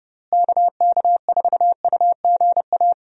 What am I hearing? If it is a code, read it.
Morse code, heard as a keyed tone, about 32 wpm, 700 Hz. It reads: KK4UGA
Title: USA Call Signs - 6 Characters - Repeated 3x - 30wpm